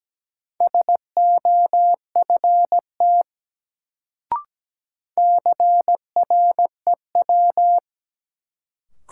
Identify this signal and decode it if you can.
Morse code, heard as a keyed tone, about 17 wpm, 700 Hz. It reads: SOFT CREW
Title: Sentences from Top 1500 Words - New Word Drill - 17wpm